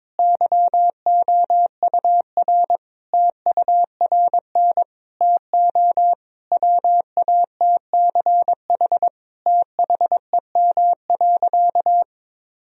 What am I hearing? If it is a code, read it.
Morse code, heard as a keyed tone, about 22 wpm, 700 Hz. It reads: YOUR TURN TO WATCH THEM.
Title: Sentences from Top 1000 Words - Review Part 1 - 22wpm